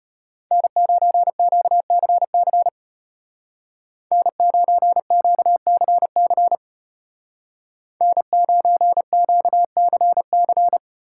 Morse code, heard as a keyed tone, about 32 wpm, 700 Hz. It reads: N9QCC N9QCC N9QCC